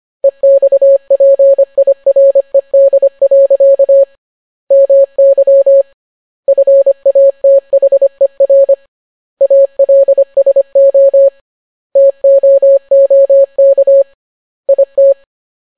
Morse code, heard as a keyed tone, about 25 wpm, 550 Hz. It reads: EXPIRED. MY FATHER ALSO TOOK IT